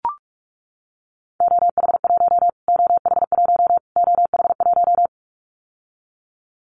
Morse code, heard as a keyed tone, about 44 wpm, 700 Hz. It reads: OH1 OH1 OH1
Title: Callsign Prefixes for Top 25 DXCC - Repeated 3x - 45wpm